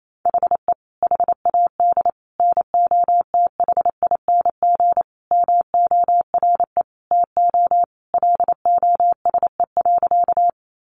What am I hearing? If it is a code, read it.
Morse code, heard as a keyed tone, about 28 wpm, 700 Hz. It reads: HE HAD NOTHING MORE TO LOSE.
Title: Sentences from Top 1100 Words 28wpm